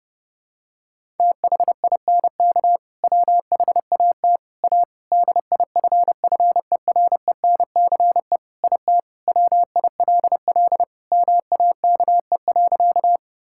Morse code, heard as a keyed tone, about 32 wpm, 700 Hz. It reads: THINK WHAT A DIFFERENCE IT WILL MAKE.
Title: Sentences from Top 1000 Words - Review Part 5 - 30wpm